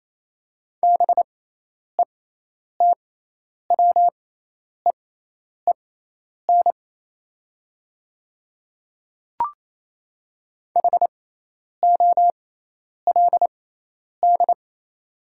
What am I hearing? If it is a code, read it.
Morse code, heard as a keyed tone, about 28 wpm, spaced as for 10 wpm, 700 Hz. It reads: BETWEEN HOLD
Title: 6x Character Spacing - Top 500 Words - 28wpm